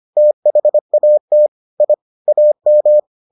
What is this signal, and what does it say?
Morse code, heard as a keyed tone, about 25 wpm, 600 Hz. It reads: THAT I AM